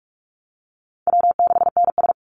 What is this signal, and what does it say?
Morse code, heard as a keyed tone, about 44 wpm, 700 Hz. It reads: W6NS
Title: Easy Call Signs Repeated 2x - 45wpm